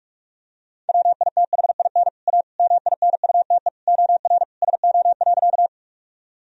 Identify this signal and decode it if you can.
Morse code, heard as a keyed tone, about 41 wpm, 700 Hz. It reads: WITHIN A MINUTE OR SO.
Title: Sentences from Top 1000 Words - Review Part 4 - 45wpm